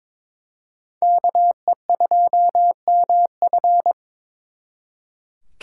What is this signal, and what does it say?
Morse code, heard as a keyed tone, about 26 wpm, 700 Hz. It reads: KE2MF
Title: Speed Racing - USA 2x2 Call Signs 22wpm